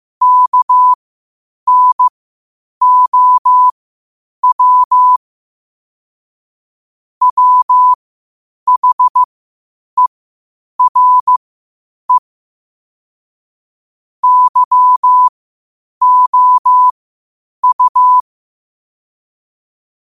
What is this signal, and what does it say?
Morse code, heard as a keyed tone, about 15 wpm, spaced as for 8 wpm, 1000 Hz. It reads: KNOW WHERE YOU
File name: qotdfiles / 8wpm